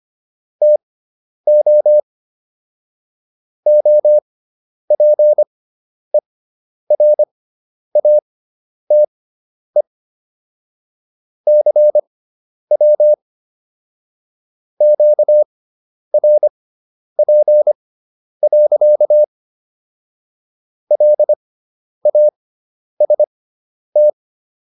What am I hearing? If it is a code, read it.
Morse code, heard as a keyed tone, about 25 wpm, spaced as for 10 wpm, 600 Hz. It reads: TO OPERATE CW QRP. LAST